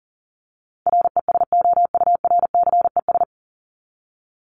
Morse code, heard as a keyed tone, about 40 wpm, 700 Hz. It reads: RESOURCES
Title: ES:S Suffix - Top 100 Words with Base Words 40wpm